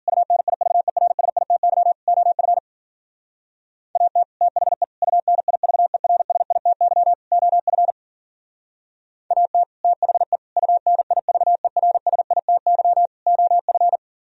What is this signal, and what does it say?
Morse code, heard as a keyed tone, about 49 wpm, 700 Hz. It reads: UNIVERSITY OF AT THE UNIVERSITY OF AT THE UNIVERSITY OF
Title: Speed-Racing - Sets of 4 Words 40wpm